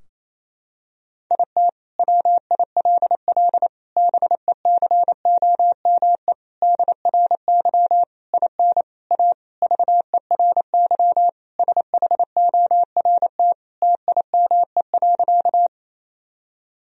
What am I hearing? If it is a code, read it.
Morse code, heard as a keyed tone, about 28 wpm, 700 Hz. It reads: IT WILL BECOME DRY IN A VERY SHORT TIME.
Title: Sentences from Top 1000 Words - Review Part 3 - 28wpm